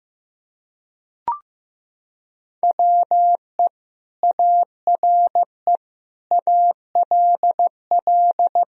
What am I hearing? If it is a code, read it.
Morse code, heard as a keyed tone, about 15 wpm, 700 Hz. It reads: WE ARE ALL
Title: Sentences from Top 1000 Words - Review Part 4 - 15wpm